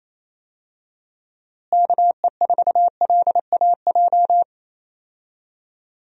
Morse code, heard as a keyed tone, about 28 wpm, 700 Hz. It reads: KE4LAJ
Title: USA Call Signs - 6 Characters - Repeated 3x - 28wpm